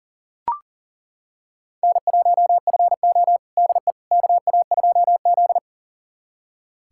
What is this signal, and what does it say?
Morse code, heard as a keyed tone, about 40 wpm, 700 Hz. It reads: N1FO DE KA2Z
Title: USA Callsign DE Callsign - 4 Characters - 40wpm